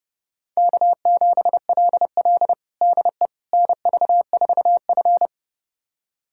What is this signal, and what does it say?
Morse code, heard as a keyed tone, about 32 wpm, 700 Hz. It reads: K7LL DE NV4F
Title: USA Callsign DE Callsign - 4 Characters - 30wpm